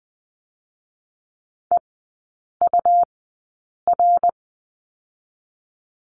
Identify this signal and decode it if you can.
Morse code, heard as a keyed tone, about 20 wpm, spaced as for 8 wpm, 700 Hz. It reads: EUR
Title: 205 Lesson 33e - Letter j - Three Characters 20wpm